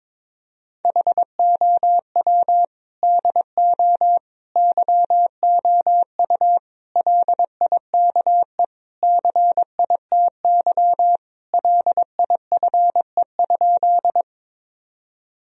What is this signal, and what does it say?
Morse code, heard as a keyed tone, about 22 wpm, 700 Hz. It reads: HOW DO YOU LIKE CITY LIFE?
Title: Sentences from Top 1000 Words - Review Part 1 - 22wpm